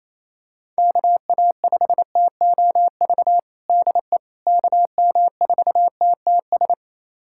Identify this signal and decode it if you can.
Morse code, heard as a keyed tone, about 28 wpm, 700 Hz. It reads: KA5TOV DE KM4TTS
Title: USA Callsign DE Callsign - 6 Characters - 28wpm